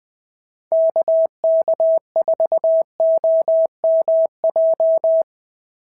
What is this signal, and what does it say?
Morse code, heard as a keyed tone, about 20 wpm, 650 Hz. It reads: KK4OMJ
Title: POTA QSO Practice - 20wpm